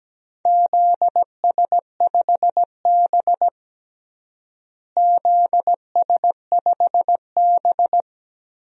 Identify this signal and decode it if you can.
Morse code, heard as a keyed tone, about 17 wpm, 700 Hz. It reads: ZS5B ZS5B
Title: DXCC-100 Call Signs - 4 Characters - Repeated 2x - 17wpm